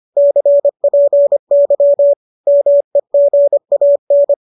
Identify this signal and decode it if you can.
Morse code, heard as a keyed tone, about 25 wpm, 580 Hz. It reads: CPY MEGAN